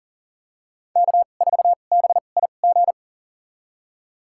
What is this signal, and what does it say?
Morse code, heard as a keyed tone, about 40 wpm, spaced as for 29 wpm, 700 Hz. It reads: K4BIG
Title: USA Call Signs - 5 Characters - 2x Character Spacing - 40wpm